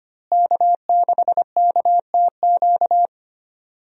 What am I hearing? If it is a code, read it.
Morse code, heard as a keyed tone, about 25 wpm, 700 Hz. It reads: K6KTQ
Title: USA Call Signs - 5 Characters - Repeated 3x - 25wpm